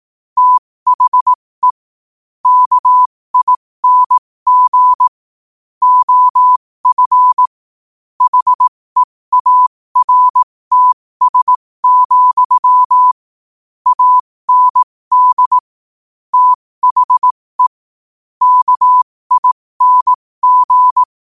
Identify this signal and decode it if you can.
Morse code, heard as a keyed tone, about 18 wpm, spaced as for 15 wpm, 1000 Hz. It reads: THE KING OF HEARTS, AND THE KING
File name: qotdfiles / 15wpm